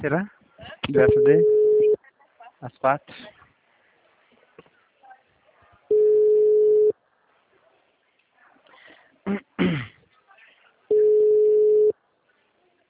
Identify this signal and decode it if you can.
Ringback tone (Europe, 425 Hz), with speech.